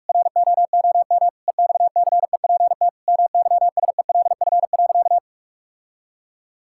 Morse code, heard as a keyed tone, about 46 wpm, 700 Hz. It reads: ROOM EXCEPT MYSELF.
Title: Sentences from Top 1600 Words 45wpm